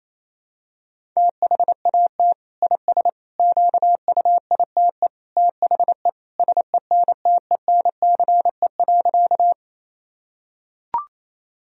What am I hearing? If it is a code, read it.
Morse code, heard as a keyed tone, about 28 wpm, 700 Hz. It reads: THAT IS QUITE THE SENTENCE.
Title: Sentences from Top 1000 Words - Review Part 1 - 28wpm